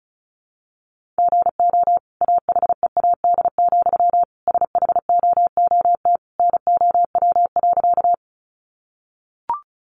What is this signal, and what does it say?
Morse code, heard as a keyed tone, about 34 wpm, 700 Hz. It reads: GO AHEAD, SHOOT NOW.
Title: Sentences from Top 1200 Words 35wpm